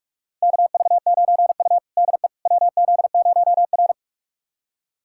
Morse code, heard as a keyed tone, about 44 wpm, 700 Hz. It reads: KV9U DE WZ0R
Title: USA Callsign DE Callsign - 4 Characters - 45wpm